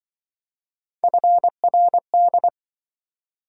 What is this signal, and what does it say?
Morse code, heard as a keyed tone, about 24 wpm, 700 Hz. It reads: FRD
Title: Speed-Racing - Three Letter-Number Combinations - 20wpm